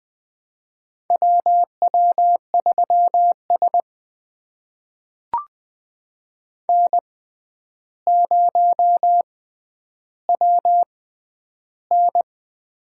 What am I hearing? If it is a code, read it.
Morse code, heard as a keyed tone, about 21 wpm, spaced as for 10 wpm, 700 Hz. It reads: WW3S N0WN
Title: USA Call Signs - 4 Characters - 6x Character Spacing - 20wpm